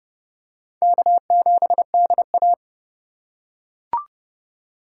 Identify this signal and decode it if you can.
Morse code, heard as a keyed tone, about 29 wpm, 700 Hz. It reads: K7DA